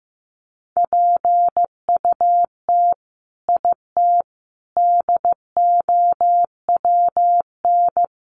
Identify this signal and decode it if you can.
Morse code, heard as a keyed tone, about 15 wpm, 700 Hz. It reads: PUT IT DOWN
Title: Sentences from Top 1400 Words 15wpm